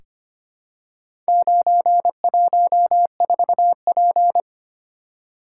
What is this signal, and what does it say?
Morse code, heard as a keyed tone, about 25 wpm, 700 Hz. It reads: 914P
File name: practice / 25wpm / Four Letter-Number Combinations - 25wpm